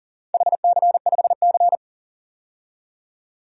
Morse code, heard as a keyed tone, about 39 wpm, 700 Hz. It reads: HC5C
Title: DXCC-100 Call Signs - 4 Characters - 40wpm